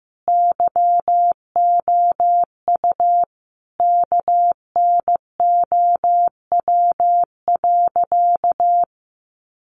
Morse code, heard as a keyed tone, about 15 wpm, 700 Hz. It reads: YOU KNOW.
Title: Sentences from Top 1300 Words 15wpm